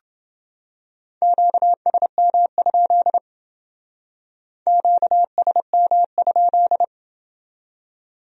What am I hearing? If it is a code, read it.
Morse code, heard as a keyed tone, about 28 wpm, 700 Hz. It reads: QSM? QSM?